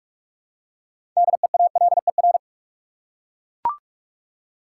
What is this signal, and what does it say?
Morse code, heard as a keyed tone, about 41 wpm, 700 Hz. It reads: DEALER
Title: Six Letter Words - 45wpm